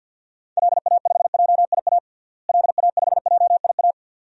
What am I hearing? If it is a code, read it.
Morse code, heard as a keyed tone, about 52 wpm, 700 Hz. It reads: LA5JIA LA5JIA
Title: DXCC-100 Call Signs - 6 Characters - Repeated 2x - 50wpm